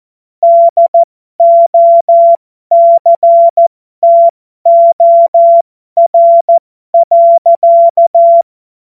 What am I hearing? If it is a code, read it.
Morse code, heard as a keyed tone, about 14 wpm, spaced as for 12 wpm, 680 Hz. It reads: DOCTOR.
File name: Morsekurs / Witze